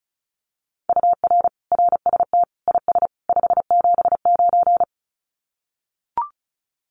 Morse code, heard as a keyed tone, about 34 wpm, 700 Hz. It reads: UR RST IS 579
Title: Common QSO Phrases 35wpm